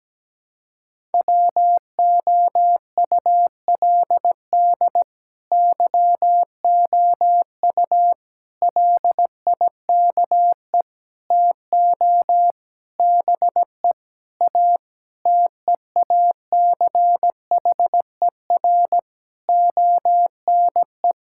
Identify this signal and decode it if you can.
Morse code, heard as a keyed tone, about 17 wpm, 700 Hz. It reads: WOULD YOU LIKE TO BE A TEACHER ONE
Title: Sentences from Top 1000 Words - Review Part 6 - 17wpm